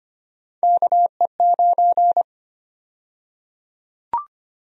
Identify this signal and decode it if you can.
Morse code, heard as a keyed tone, about 25 wpm, 700 Hz. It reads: KE9
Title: USA Call Sign Prefixes - Repeated 2x - 25wpm